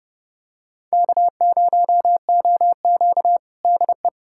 Morse code, heard as a keyed tone, about 32 wpm, 700 Hz. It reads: K0OQ DE
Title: USA Callsign DE Callsign - 4 Characters - 30wpm